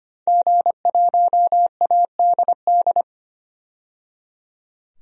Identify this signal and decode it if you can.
Morse code, heard as a keyed tone, about 25 wpm, 700 Hz. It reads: G1ADD
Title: DXCC-100 Call Signs - 5 Characters - 25wpm